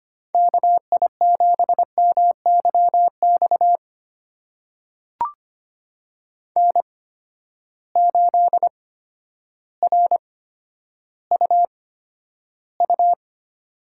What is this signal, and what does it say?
Morse code, heard as a keyed tone, about 25 wpm, spaced as for 11 wpm, 700 Hz. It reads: KI7MYX N8RUU
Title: USA Call Signs - 8x Character Spacing - 25wpm